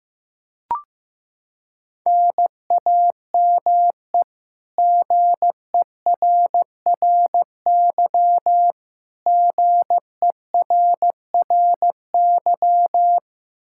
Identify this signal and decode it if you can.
Morse code, heard as a keyed tone, about 16 wpm, 700 Hz. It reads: NAME GERRY GERRY
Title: Practice QSO - Exchange 1 - Beginning - 15wpm